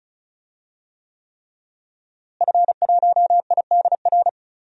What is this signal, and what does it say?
Morse code, heard as a keyed tone, about 34 wpm, 700 Hz. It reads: F1IDR
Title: DXCC-100 Call Signs - 5 Characters - Repeated 3x - 35wpm